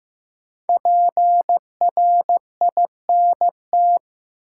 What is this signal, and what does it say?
Morse code, heard as a keyed tone, about 15 wpm, 700 Hz. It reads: PRINT
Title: Sentences from Top 1400 Words - New Word Drill Alt - 15wpm